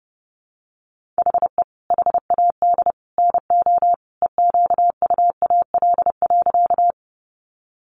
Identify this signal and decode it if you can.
Morse code, heard as a keyed tone, about 30 wpm, 700 Hz. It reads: HE HAD NO EQUAL.
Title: Sentences from Top 1400 Words 30wpm